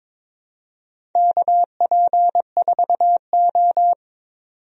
Morse code, heard as a keyed tone, about 21 wpm, 700 Hz. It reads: KP4O
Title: DXCC-100 Call Signs - 4 Characters - 22wpm